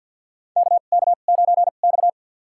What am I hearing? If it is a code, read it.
Morse code, heard as a keyed tone, about 50 wpm, spaced as for 36 wpm, 700 Hz. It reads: KK9X
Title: USA Call Signs - 2x Character Spacing - 50wpm